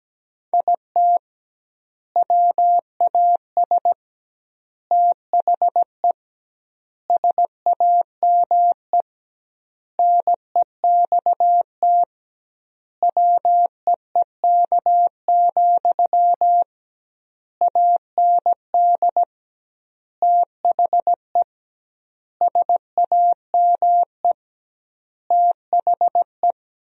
Morse code, heard as a keyed tone, about 17 wpm, 700 Hz. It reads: IT WAS THE SAME NEXT WEEK, AND THE SAME THE
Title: 2x Word-Spacing - Sentences from Top 500 Words - 17wpm